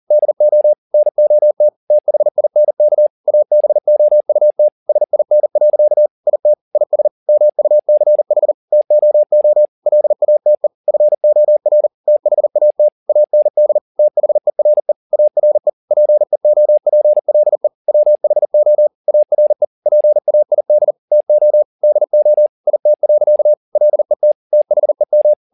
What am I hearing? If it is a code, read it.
Morse code, heard as a keyed tone, about 40 wpm, 600 Hz. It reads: DO NOT THINK ABOUT SIN. IT IS MUCH TOO LATE FOR THAT AND THERE ARE PEOPLE WHO ARE PAID TO DO IT. LET THEM
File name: the old man and the sea 2 40wpm